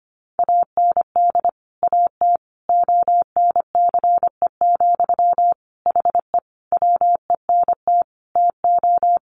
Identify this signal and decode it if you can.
Morse code, heard as a keyed tone, about 25 wpm, 700 Hz. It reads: AND AT ONCE, HE WENT TO
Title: Sentences from Top 1300 Words 25wpm